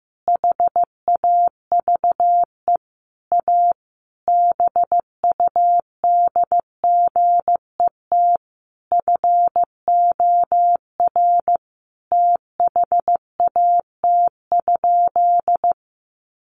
Morse code, heard as a keyed tone, about 15 wpm, 700 Hz. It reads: HAVE A BUDGET FOR THAT?
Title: Sentences from Top 1200 Words 15wpm